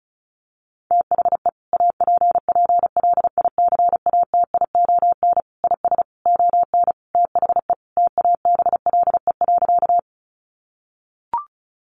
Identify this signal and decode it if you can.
Morse code, heard as a keyed tone, about 34 wpm, 700 Hz. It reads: THE APPLICATION IS ON THE TABLE.
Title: Sentences from Top 1200 Words 35wpm